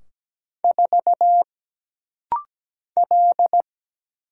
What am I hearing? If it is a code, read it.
Morse code, heard as a keyed tone, about 17 wpm, 700 Hz. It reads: 4 L
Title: Single Letter-Number - Rapid-Fire 17wpm